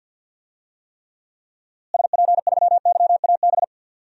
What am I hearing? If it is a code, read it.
Morse code, heard as a keyed tone, about 46 wpm, 700 Hz. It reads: SP3YAB
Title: DXCC-100 Call Signs - 6 Characters - Repeated 2x - 50wpm